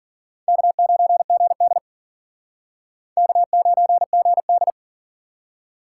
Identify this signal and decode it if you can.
Morse code, heard as a keyed tone, about 51 wpm, 700 Hz. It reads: K9GD K9GD